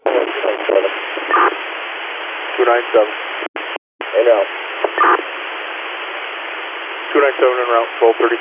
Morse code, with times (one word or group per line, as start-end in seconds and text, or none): none